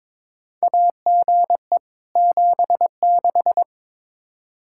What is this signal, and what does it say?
Morse code, heard as a keyed tone, about 22 wpm, 700 Hz. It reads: AGE 76
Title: Practice QSO - Exchange 3 - More Info - 22wpm